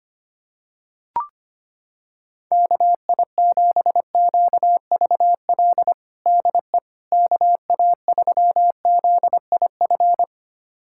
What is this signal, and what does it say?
Morse code, heard as a keyed tone, about 25 wpm, 700 Hz. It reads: KI7QVL DE KA3ZIF
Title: USA Callsign DE Callsign 25wpm